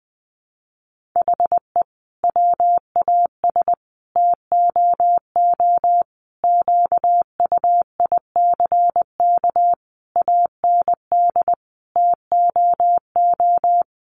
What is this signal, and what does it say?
Morse code, heard as a keyed tone, about 20 wpm, 700 Hz. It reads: HE WAS TOO QUICK AND TOO
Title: Sentences from Top 600 Words 20wpm